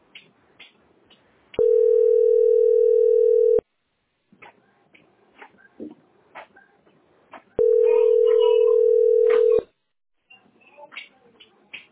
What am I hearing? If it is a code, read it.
Ringback tone (North America), 440 + 480 Hz.